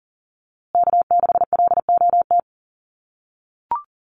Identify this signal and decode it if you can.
Morse code, heard as a keyed tone, about 40 wpm, 700 Hz. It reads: K6LOT
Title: USA Call Signs Repeated 2x 40wpm